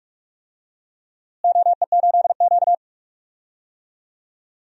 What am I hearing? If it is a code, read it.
Morse code, heard as a keyed tone, about 47 wpm, 700 Hz. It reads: OE8Q
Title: DXCC-100 Call Signs - 4 Characters - Repeated 2x - 45wpm